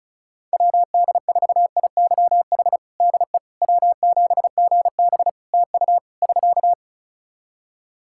Morse code, heard as a keyed tone, about 36 wpm, 700 Hz. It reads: WD4IYH DE W7GB TU <SK>
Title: Practice QSO - Exchange 4 - Ending - 35wpm